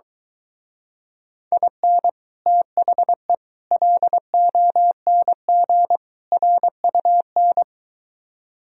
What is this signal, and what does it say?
Morse code, heard as a keyed tone, about 23 wpm, 700 Hz. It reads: IN THE LONG RUN